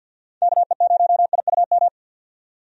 Morse code, heard as a keyed tone, about 48 wpm, 700 Hz. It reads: KE0IUM
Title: USA Call Signs - 6 Characters - 50wpm